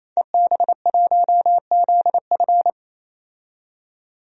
Morse code, heard as a keyed tone, about 28 wpm, 700 Hz. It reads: EB1ZF
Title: DXCC-100 Call Signs - 5 Characters - 28wpm